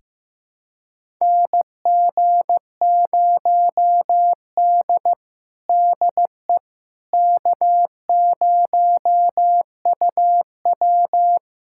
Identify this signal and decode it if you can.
Morse code, heard as a keyed tone, about 15 wpm, 700 Hz. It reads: NG0D DE K0UW
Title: USA Callsign DE Callsign - 4 Characters - 15wpm